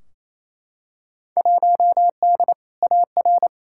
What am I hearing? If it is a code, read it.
Morse code, heard as a keyed tone, about 28 wpm, 700 Hz. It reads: 1D AR
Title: Field Day Exchanges Encore - 28wpm